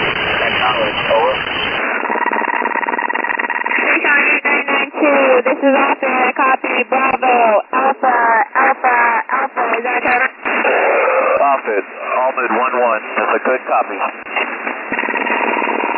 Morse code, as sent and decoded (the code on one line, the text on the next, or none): none